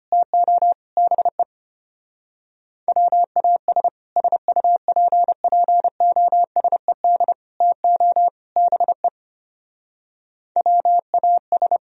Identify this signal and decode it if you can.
Morse code, heard as a keyed tone, about 29 wpm, 700 Hz. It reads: TO BE WAS SUPPOSED TO BE WAS
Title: Speed-Racing - Sets of 4 Words 25wpm